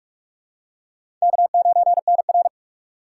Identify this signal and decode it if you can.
Morse code, heard as a keyed tone, about 42 wpm, 700 Hz. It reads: K9NR
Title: K1USN SST QSO Practice - 45wpm